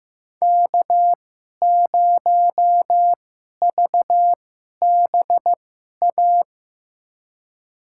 Morse code, heard as a keyed tone, about 15 wpm, spaced as for 11 wpm, 700 Hz. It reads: K0VBA
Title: USA Call Signs - 5 Characters - 2x Character Spacing - 15wpm